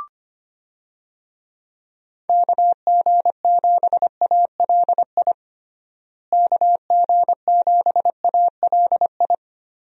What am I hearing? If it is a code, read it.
Morse code, heard as a keyed tone, about 25 wpm, 700 Hz. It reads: KG7ALI KG7ALI
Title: USA Call Signs - 6 Characters - Repeated 2x - 25wpm